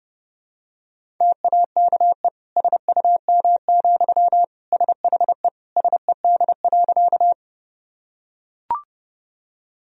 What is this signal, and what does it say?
Morse code, heard as a keyed tone, about 32 wpm, 700 Hz. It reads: TAKE SUM, SHE SED.
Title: Sentences from Top 200 Words and Abbreviations 30wpm